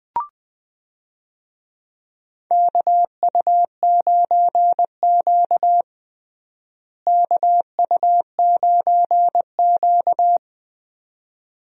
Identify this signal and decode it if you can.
Morse code, heard as a keyed tone, about 21 wpm, 700 Hz. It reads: KU9Q KU9Q
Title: USA Call Signs - 4 Characters - Repeated 2x - 20wpm